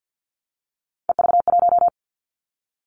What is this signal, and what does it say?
Morse code, heard as a keyed tone, about 50 wpm, 700 Hz. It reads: EV1
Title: Three Challenging Letter-Number Combinations 50wpm